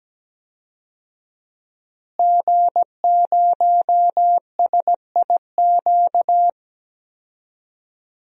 Morse code, heard as a keyed tone, about 17 wpm, 700 Hz. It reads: G0SIQ
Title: DXCC-100 Call Signs - 5 Characters - Repeated 2x - 17wpm